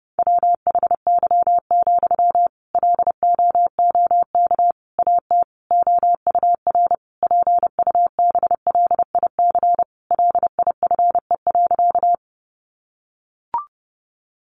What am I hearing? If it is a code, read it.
Morse code, heard as a keyed tone, about 30 wpm, 700 Hz. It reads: WHY, LOOK AT OUR PUBLIC LIFE.
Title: Sentences from Top 600 Words 30wpm